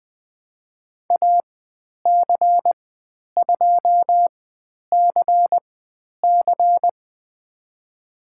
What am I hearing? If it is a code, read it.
Morse code, heard as a keyed tone, about 20 wpm, spaced as for 10 wpm, 700 Hz. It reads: AC2CC
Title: 102 Lesson 17g - Letter u - Call Signs 20wpm